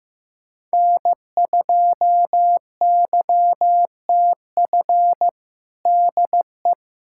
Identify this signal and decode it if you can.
Morse code, heard as a keyed tone, about 15 wpm, 700 Hz. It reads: N2YTF DE
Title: USA Callsign DE Callsign - 5 Characters - 15wpm